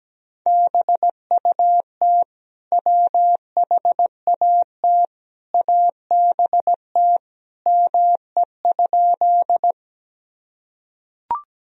Morse code, heard as a keyed tone, about 17 wpm, 700 Hz. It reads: BUT WHAT ABT ME?
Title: Sentences from Top 200 Words and Abbreviations 17wpm